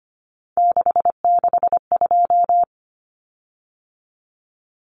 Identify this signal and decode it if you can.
Morse code, heard as a keyed tone, about 25 wpm, 700 Hz. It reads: 662
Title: Common Cut Numbers - 3-Digit 25wpm